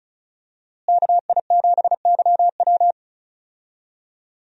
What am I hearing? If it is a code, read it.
Morse code, heard as a keyed tone, about 34 wpm, 700 Hz. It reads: KI7YW
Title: USA Call Signs - 5 Characters - Repeated 3x - 35wpm